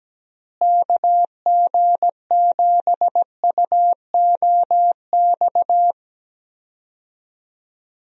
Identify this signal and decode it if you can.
Morse code, heard as a keyed tone, about 17 wpm, 700 Hz. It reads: KG7UOX
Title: USA Call Signs - 6 Characters - Repeated 2x - 17wpm